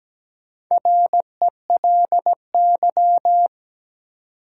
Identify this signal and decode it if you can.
Morse code, heard as a keyed tone, about 17 wpm, 700 Hz. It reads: RELY
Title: Four Letter Words - 17wpm